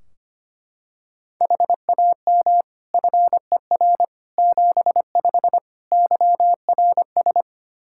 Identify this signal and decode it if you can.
Morse code, heard as a keyed tone, about 25 wpm, 700 Hz. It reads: HAM FER 75 YRS
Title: Practice QSO - Exchange 3 - More Info - 25wpm